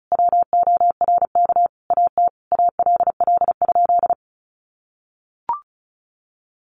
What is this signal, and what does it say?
Morse code, heard as a keyed tone, about 34 wpm, 700 Hz. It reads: WORK AT ALL?
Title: Sentences from Top 1500 Words 35wpm